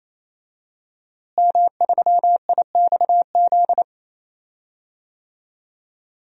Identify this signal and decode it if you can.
Morse code, heard as a keyed tone, about 28 wpm, 700 Hz. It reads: M3IXZ
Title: DXCC-100 Call Signs - 5 Characters - Repeated 3x - 28wpm